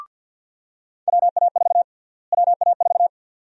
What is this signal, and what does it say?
Morse code, heard as a keyed tone, about 48 wpm, 700 Hz. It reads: WA4 WA4
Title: USA Call Sign Prefixes - Repeated 2x - 50wpm